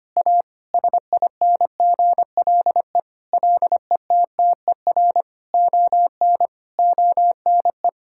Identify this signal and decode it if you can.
Morse code, heard as a keyed tone, about 25 wpm, 700 Hz. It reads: A SINGLE LETTER ON ONE
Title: Sentences from Top 1600 Words 25wpm